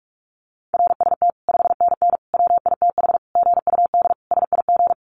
Morse code, heard as a keyed tone, about 44 wpm, 700 Hz. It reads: RST 5NN WITH GUD SIG
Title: Common QSO Phrases 45wpm